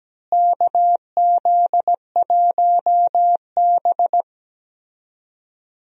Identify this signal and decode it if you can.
Morse code, heard as a keyed tone, about 17 wpm, 700 Hz. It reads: KZ1B
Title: USA Call Signs - 4 Characters - Repeated 3x - 17wpm